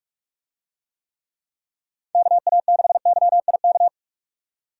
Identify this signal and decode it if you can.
Morse code, heard as a keyed tone, about 45 wpm, 700 Hz. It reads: KA6YIK